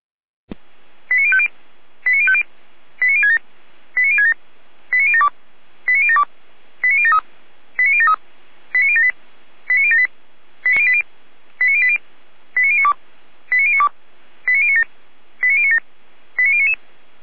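Five-tone selective call (ZVEI), heard as quick runs of five tones, tones of about 70 ms. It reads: 89050 89050 89067 89067 89072 89072 89073 89073 89078 89078 89080 89092 89092 89097 89097 89099
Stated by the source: Five-tone ZVEI selective call received on SDR.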